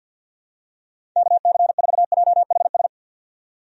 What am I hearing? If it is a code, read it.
Morse code, heard as a keyed tone, about 47 wpm, 700 Hz. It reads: KC4JHS
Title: USA Call Signs - 6 Characters - 50wpm